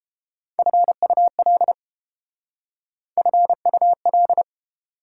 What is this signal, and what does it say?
Morse code, heard as a keyed tone, about 30 wpm, 690 Hz. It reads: FUL FUL